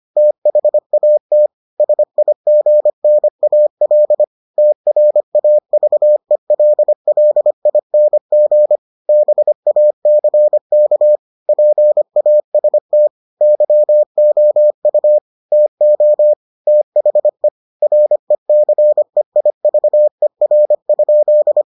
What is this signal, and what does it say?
Morse code, heard as a keyed tone, about 25 wpm, 600 Hz. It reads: THAT SIGNAL TRAVELLING BACK PAST YOU TO THE RECEIVER?